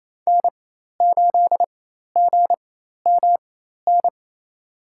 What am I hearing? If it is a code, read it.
Morse code, heard as a keyed tone, about 28 wpm, spaced as for 13 wpm, 700 Hz. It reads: N8GMN
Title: USA Call Signs - 5 Characters - 4x Character Spacing - 28wpm